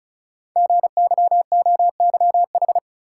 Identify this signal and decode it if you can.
Morse code, heard as a keyed tone, about 36 wpm, 690 Hz. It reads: GYOYH
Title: Code Groups - 5 Letter:Numbers - 35wpm